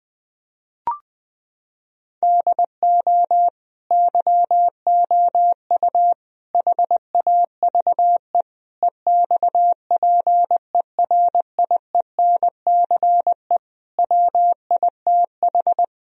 Morse code, heard as a keyed tone, about 19 wpm, 700 Hz. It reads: DO YOU HAVE EXPERIENCE WITH
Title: Sentences from Top 1000 Words - Review Part 3 - 20wpm